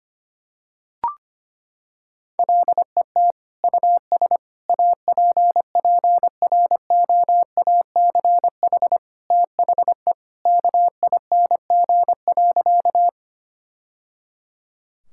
Morse code, heard as a keyed tone, about 25 wpm, 700 Hz. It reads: LET US APPROACH THE KING.
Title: Sentences from Top 1000 Words - Review Part 6 - 25wpm